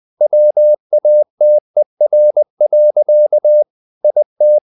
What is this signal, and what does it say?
Morse code, heard as a keyed tone, about 20 wpm, 600 Hz. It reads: WATER. IT
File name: the old man and the sea part 2 20wpm